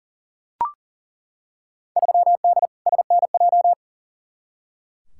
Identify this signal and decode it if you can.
Morse code, heard as a keyed tone, about 40 wpm, 700 Hz. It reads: 3D SNJ
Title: Field Day Exchanges Encore - 40wpm